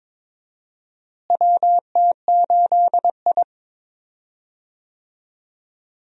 Morse code, heard as a keyed tone, about 21 wpm, 700 Hz. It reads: WT8I
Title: USA Call Signs - 4 Characters - Repeated 2x - 22wpm